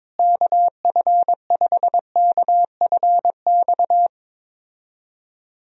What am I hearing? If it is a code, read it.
Morse code, heard as a keyed tone, about 22 wpm, 700 Hz. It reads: KF5KFX
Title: USA Call Signs - 6 Characters - Repeated 3x - 22wpm